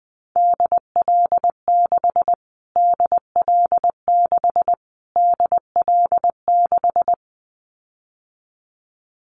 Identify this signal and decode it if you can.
Morse code, heard as a keyed tone, about 20 wpm, 700 Hz. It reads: DL6 DL6 DL6
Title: Callsign Prefixes for Top 25 DXCC - Repeated 3x - 20wpm